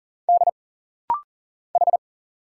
Morse code, heard as a keyed tone, about 40 wpm, 700 Hz. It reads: D H
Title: Single Letters - Rapid-Fire 40wpm